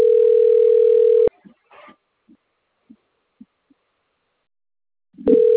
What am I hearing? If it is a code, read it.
Ringback tone (North America), 440 + 480 Hz.
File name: out- - - - - .32681